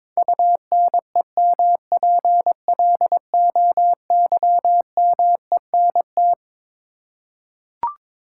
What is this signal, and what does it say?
Morse code, heard as a keyed tone, about 21 wpm, 700 Hz. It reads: UNEMPLOYMENT
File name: Twelve Letter Words - 22wpm